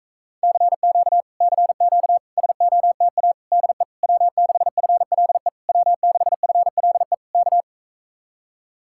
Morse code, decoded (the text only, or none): CQ CQ SOTA DE W6FLE W6FLE K